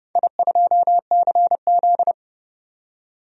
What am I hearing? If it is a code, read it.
Morse code, heard as a keyed tone, about 30 wpm, 700 Hz. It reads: I2CZ
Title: DXCC-100 Call Signs - 4 Characters - 30wpm